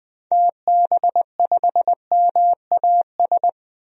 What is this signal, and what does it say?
Morse code, heard as a keyed tone, about 20 wpm, 700 Hz. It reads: TB5MAS
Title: DXCC-100 Call Signs - 6 Characters - 20wpm